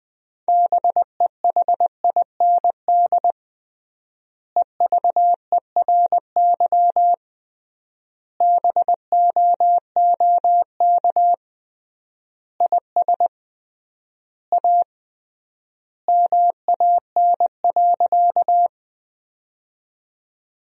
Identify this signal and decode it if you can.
Morse code, heard as a keyed tone, about 20 wpm, 700 Hz. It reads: BEHIND EVERY BOOK IS A MAN.
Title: 3x Word-Spacing - Sentences from Top 500 Words - 20wpm